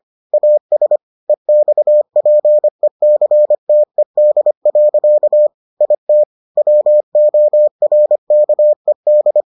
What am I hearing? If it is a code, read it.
Morse code, heard as a keyed tone, about 25 wpm, 600 Hz. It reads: AS EXPECTED. IT WORKED